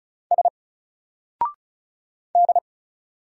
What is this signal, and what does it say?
Morse code, heard as a keyed tone, about 36 wpm, 700 Hz. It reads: S D
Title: Single Character (Letter:Number:Punctuation:Prosign) Alt 35wpm